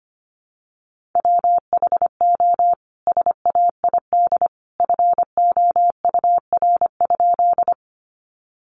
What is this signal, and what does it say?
Morse code, heard as a keyed tone, about 25 wpm, 700 Hz. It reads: WHO SAID FOUR?
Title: Sentences from Top 300 Words 25wpm